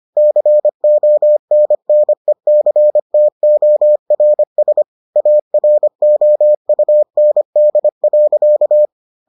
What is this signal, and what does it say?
Morse code, heard as a keyed tone, about 25 wpm, 600 Hz. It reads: CONNECTORS AROUND.